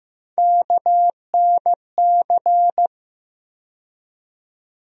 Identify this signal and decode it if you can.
Morse code, heard as a keyed tone, about 15 wpm, 700 Hz. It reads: KNC
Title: USA Call Sign Postfixes - 3 Letters 15wpm